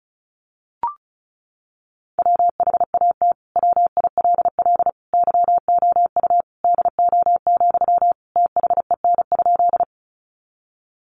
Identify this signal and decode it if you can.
Morse code, heard as a keyed tone, about 34 wpm, 700 Hz. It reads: WHAT WILL YOU DO, THEN?
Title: Sentences from Top 100 Words 35wpm